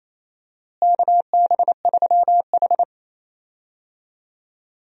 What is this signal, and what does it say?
Morse code, heard as a keyed tone, about 27 wpm, 700 Hz. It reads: KB3H
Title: USA Call Signs - 4 Characters - Repeated 3x - 28wpm